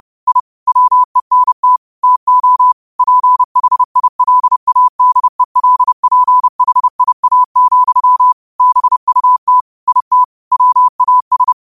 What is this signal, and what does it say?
Morse code, heard as a keyed tone, about 30 wpm, 1000 Hz. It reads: I WENT TO PHILADELPHIA, BUT IT WAS
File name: qotdfiles / 30wpm